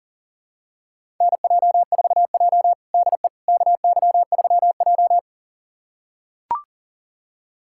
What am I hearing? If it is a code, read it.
Morse code, heard as a keyed tone, about 40 wpm, 700 Hz. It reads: NJ4J DE KY3J
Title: USA Callsign DE Callsign - 4 Characters - 40wpm